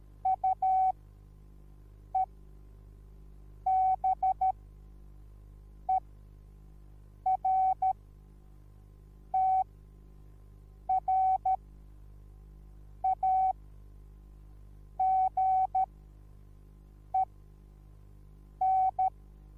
Morse code, heard as a keyed tone, about 13 wpm, spaced as for 5 wpm, 760 Hz. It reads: UEBERTRAGEN
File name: Onlinelehrgang / morsen / DJ4UF Morsen21